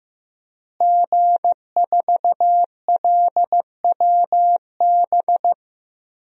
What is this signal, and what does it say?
Morse code, heard as a keyed tone, about 15 wpm, 700 Hz. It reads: G4LWB
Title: DXCC-100 Call Signs - 5 Characters - Repeated 3x - 15wpm